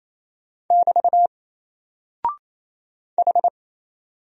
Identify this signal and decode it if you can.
Morse code, heard as a keyed tone, about 27 wpm, 700 Hz. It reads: <BT> H
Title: Single Character (Letter:Number:Punctuation:Prosign) 28wpm